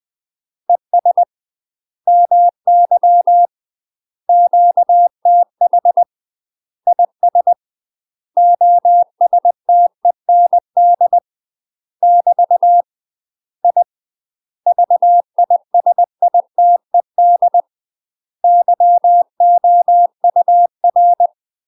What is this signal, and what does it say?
Morse code, heard as a keyed tone, about 20 wpm, 700 Hz. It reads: ES MY QTH IS OSTEND <BT> I VISITED YOUR
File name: UK MASTER QSOs TRAINING / 20WPM / UK MASTER QSOs 0042